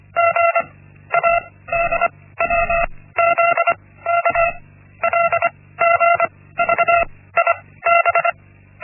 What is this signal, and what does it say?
Morse code, heard as a keyed tone, about 25 wpm, 680 Hz. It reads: G A D W Z K L G V I B
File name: Linksmakalnis-1